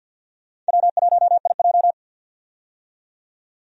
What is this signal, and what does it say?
Morse code, heard as a keyed tone, about 51 wpm, 700 Hz. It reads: W1IJ
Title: USA Call Signs - 4 Characters - 50wpm